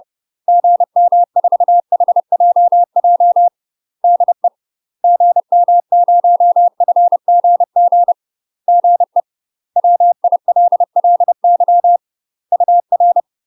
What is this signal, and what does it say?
Morse code, heard as a keyed tone, about 30 wpm, 700 Hz. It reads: GM4HJJ DE GM0FGG GE WILLY UR
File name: UK MASTER QSOs TRAINING / 30WPM / UK MASTER QSOs 0034